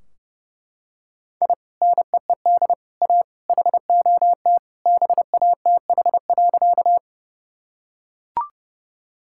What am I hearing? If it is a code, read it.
Morse code, heard as a keyed tone, about 30 wpm, 700 Hz. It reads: I NEED A HOT BATH.
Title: Sentences from Top 1600 Words 30wpm